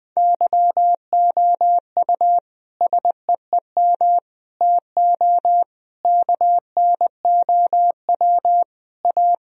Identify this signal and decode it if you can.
Morse code, heard as a keyed tone, about 20 wpm, 700 Hz. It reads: YOU SEEM TO KNOW A
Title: Sentences from Top 1000 Words - Review Part 1 - 20wpm